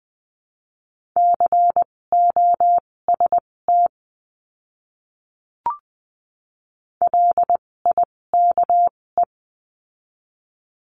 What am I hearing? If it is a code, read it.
Morse code, heard as a keyed tone, about 20 wpm, spaced as for 16 wpm, 700 Hz. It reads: COST LIKE
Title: 256 Lesson 42a - Words 12-20wpm